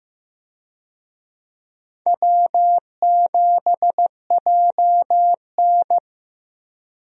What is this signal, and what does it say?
Morse code, heard as a keyed tone, about 16 wpm, 700 Hz. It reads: W7JN
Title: USA Call Signs - 4 Characters - Repeated 3x - 15wpm